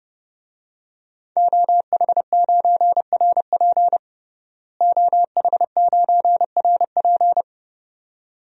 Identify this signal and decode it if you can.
Morse code, heard as a keyed tone, about 30 wpm, 700 Hz. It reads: OH9RP OH9RP
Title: DXCC-100 Call Signs - 5 Characters - Repeated 2x - 30wpm